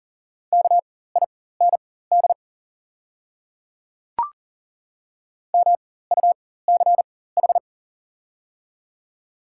Morse code, heard as a keyed tone, about 40 wpm, spaced as for 19 wpm, 700 Hz. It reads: KIND MUCH
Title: 4x Character Spacing - Top 200 Words - 40wpm